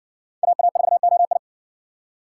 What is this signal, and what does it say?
Morse code, heard as a keyed tone, about 56 wpm, 700 Hz. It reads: AA4OI